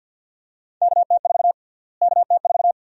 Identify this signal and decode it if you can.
Morse code, heard as a keyed tone, about 51 wpm, 700 Hz. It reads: KT4 KT4
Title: USA Call Sign Prefixes - Repeated 2x - 50wpm